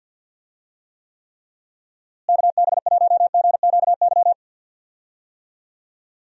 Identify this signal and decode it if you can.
Morse code, heard as a keyed tone, about 46 wpm, 700 Hz. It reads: KB1GQY
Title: USA Call Signs - 6 Characters - Repeated 3x - 50wpm